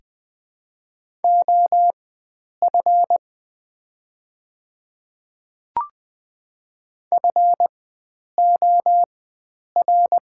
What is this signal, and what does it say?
Morse code, heard as a keyed tone, about 20 wpm, spaced as for 9 wpm, 700 Hz. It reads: OF FOR
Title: 4x Character Spacing - Top 100 Words - 20wpm